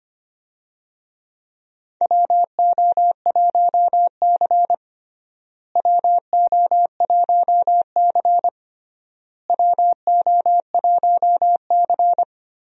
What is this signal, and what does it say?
Morse code, heard as a keyed tone, about 25 wpm, 700 Hz. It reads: WO1C WO1C WO1C
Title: USA Call Signs - 4 Characters - Repeated 3x - 25wpm